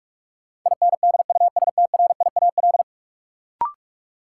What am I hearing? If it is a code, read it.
Morse code, heard as a keyed tone, about 46 wpm, 700 Hz. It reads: INDUSTRIAL
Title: Sets of 5 Words - New Word Drill - 45wpm